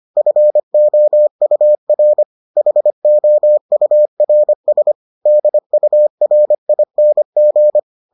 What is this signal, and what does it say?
Morse code, heard as a keyed tone, about 25 wpm, 600 Hz. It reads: FOUR HOURS DURING